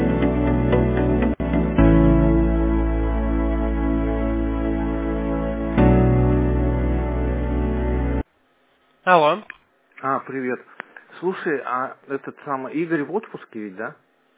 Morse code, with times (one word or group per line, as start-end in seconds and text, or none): none